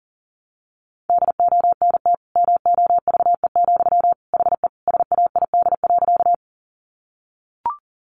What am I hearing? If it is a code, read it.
Morse code, heard as a keyed tone, about 40 wpm, 700 Hz. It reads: DONT MOVE, HE SAID.
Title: Sentences from Top 300 Words 40wpm